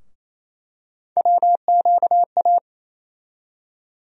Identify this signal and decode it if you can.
Morse code, heard as a keyed tone, about 28 wpm, 700 Hz. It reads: WQA
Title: USA Call Sign Postfixes - 3 Letters 28wpm